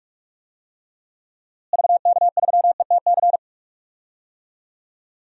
Morse code, heard as a keyed tone, about 46 wpm, 700 Hz. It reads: VK3ETC
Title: DXCC-100 Call Signs - 6 Characters - Repeated 3x - 45wpm